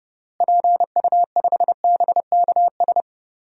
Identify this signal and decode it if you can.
Morse code, heard as a keyed tone, about 30 wpm, 700 Hz. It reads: PU5BKS
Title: DXCC-100 Call Signs - 6 Characters - Repeated 3x - 30wpm